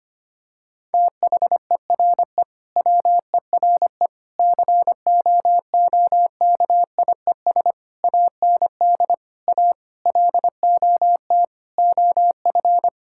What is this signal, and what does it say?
Morse code, heard as a keyed tone, about 25 wpm, 700 Hz. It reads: THERE WERE COOKIES AND A LOT OF
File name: practice / 25wpm / Sentences with 4-Word Phrases - 25wpm